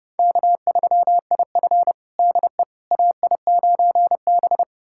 Morse code, heard as a keyed tone, about 32 wpm, 700 Hz. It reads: K3IF DE AI9B
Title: USA Callsign DE Callsign - 4 Characters - 30wpm